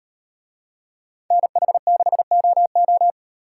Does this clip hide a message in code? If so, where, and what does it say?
Yes: Morse code from 1.3 to 3.1 s, heard as a keyed tone, about 38 wpm, 700 Hz. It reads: NH6OO